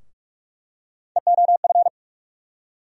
Morse code, heard as a keyed tone, about 42 wpm, 700 Hz. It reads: EOF
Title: USA Call Sign Postfixes - 3 Letters 45wpm